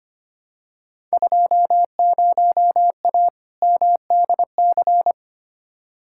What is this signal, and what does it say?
Morse code, heard as a keyed tone, about 25 wpm, 700 Hz. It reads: 20A MDC
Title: Field Day Exchanges Encore - 25wpm